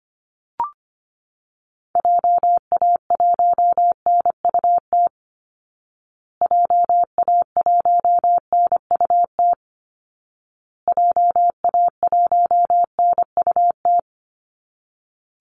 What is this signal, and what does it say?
Morse code, heard as a keyed tone, about 25 wpm, 700 Hz. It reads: JA1NUT JA1NUT JA1NUT
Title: Call Signs Repeated 2x - 25wpm